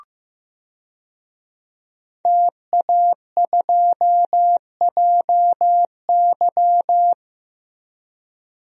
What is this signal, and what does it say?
Morse code, heard as a keyed tone, about 15 wpm, 700 Hz. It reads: TA2JY
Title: DXCC-100 Call Signs - 5 Characters - Repeated 2x - 15wpm